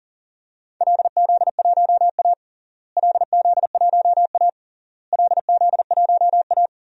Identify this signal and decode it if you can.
Morse code, heard as a keyed tone, about 37 wpm, 700 Hz. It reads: LZ1A LZ1A LZ1A
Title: DXCC-100 Call Signs - 4 Characters - Repeated 3x - 40wpm